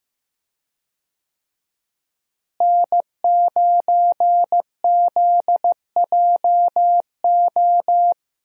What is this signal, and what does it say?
Morse code, heard as a keyed tone, about 15 wpm, 700 Hz. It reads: N9ZJO
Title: USA Call Signs - 5 Characters - Repeated 3x - 15wpm